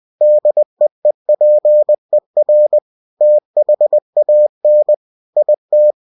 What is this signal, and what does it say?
Morse code, heard as a keyed tone, about 20 wpm, 600 Hz. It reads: DEEPER THAN IT
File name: cw-books / winnie-the-pooh / winnie-20wpm-page-0040